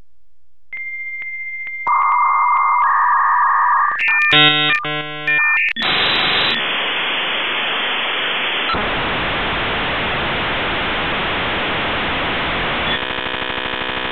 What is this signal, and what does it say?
Dial-up modem handshake.